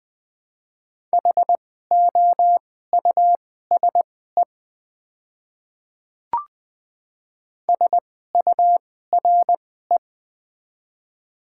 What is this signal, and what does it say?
Morse code, heard as a keyed tone, about 20 wpm, spaced as for 14 wpm, 700 Hz. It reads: HOUSE SURE
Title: 2x Character Spacing - Top 500 Words - 20wpm